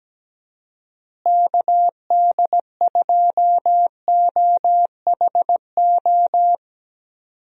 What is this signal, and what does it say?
Morse code, heard as a keyed tone, about 17 wpm, 700 Hz. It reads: KD2OHO
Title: USA Call Signs - 6 Characters - Repeated 2x - 17wpm